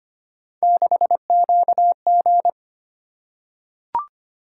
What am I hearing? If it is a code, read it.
Morse code, heard as a keyed tone, about 25 wpm, 700 Hz. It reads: 6QG
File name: Three Letter-Number Combinations - 25wpm